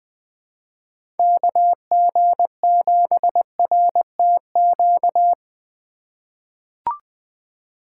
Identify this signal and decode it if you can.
Morse code, heard as a keyed tone, about 20 wpm, 700 Hz. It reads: KG7RTQ
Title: USA Call Signs - 6 Characters - 2x Character Spacing - 20wpm